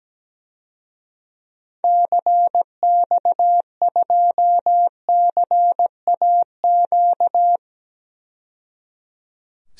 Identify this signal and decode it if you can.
Morse code, heard as a keyed tone, about 17 wpm, 700 Hz. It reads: CX2CAQ
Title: DXCC-100 Call Signs - 6 Characters - 17wpm